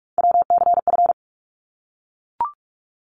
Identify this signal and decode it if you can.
Morse code, heard as a keyed tone, about 44 wpm, 700 Hz. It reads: WCF
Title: ARRL Sections 45wpm